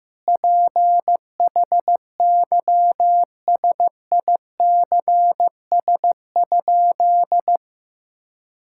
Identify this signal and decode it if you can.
Morse code, heard as a keyed tone, about 15 wpm, 700 Hz. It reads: PHYSICS?
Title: Sentences from Top 1000 Words - Review Part 5 - 15wpm